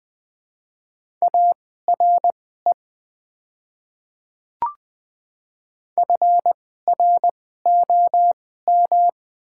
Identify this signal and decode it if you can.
Morse code, heard as a keyed tone, about 20 wpm, spaced as for 15 wpm, 700 Hz. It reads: ARE FROM
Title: 2x Character Spacing - Top 100 Words - 20wpm